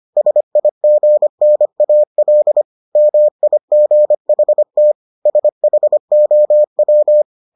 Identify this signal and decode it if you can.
Morse code, heard as a keyed tone, about 25 wpm, 600 Hz. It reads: SIGNAL MIGHT SHOW